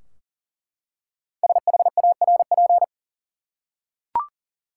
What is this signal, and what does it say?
Morse code, heard as a keyed tone, about 40 wpm, 700 Hz. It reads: SHARP
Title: Five Letter Words - 40wpm